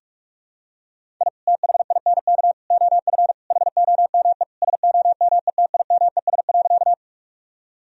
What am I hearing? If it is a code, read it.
Morse code, heard as a keyed tone, about 42 wpm, 700 Hz. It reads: I THINK OF HOME SOMETIMES.
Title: Sentences from Top 1000 Words - Review Part 3 - 45wpm